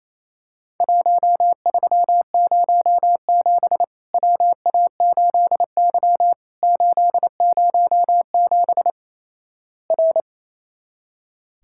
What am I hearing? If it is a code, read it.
Morse code, heard as a keyed tone, about 28 wpm, 700 Hz. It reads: 1307 WA8Y 807 R
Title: Worked All Europe DX Contest - QTC Exchange - 28wpm